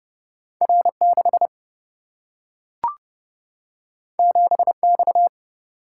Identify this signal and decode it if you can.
Morse code, heard as a keyed tone, about 30 wpm, 700 Hz. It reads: R 6 7 X
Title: Single-Letter Single-Number 30wpm